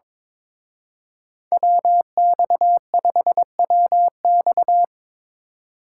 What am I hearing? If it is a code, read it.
Morse code, heard as a keyed tone, about 26 wpm, 700 Hz. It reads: WX5WX
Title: Speed Racing - USA Call Signs 22wpm